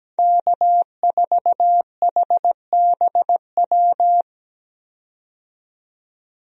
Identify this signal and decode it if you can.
Morse code, heard as a keyed tone, about 17 wpm, 700 Hz. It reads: K4HBW
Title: USA Call Signs - 5 Characters - 17wpm